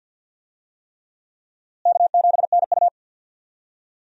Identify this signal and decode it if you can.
Morse code, heard as a keyed tone, about 51 wpm, 700 Hz. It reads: K7NU
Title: USA Call Signs - 4 Characters - Repeated 3x - 50wpm